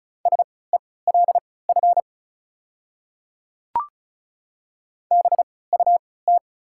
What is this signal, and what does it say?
Morse code, heard as a keyed tone, about 36 wpm, spaced as for 20 wpm, 700 Hz. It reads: SELF BUT
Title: 3x Character Spacing - Top 300 Words - 35wpm